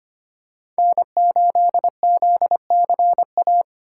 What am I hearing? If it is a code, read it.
Morse code, heard as a keyed tone, about 25 wpm, 700 Hz. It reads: N8ZCA
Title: USA Call Signs - 5 Characters - 3x Character Spacing - 25wpm